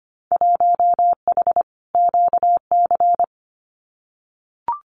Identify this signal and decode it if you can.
Morse code, heard as a keyed tone, about 25 wpm, 700 Hz. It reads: 1H QC
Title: Winter Field Day Exchanges 25wpm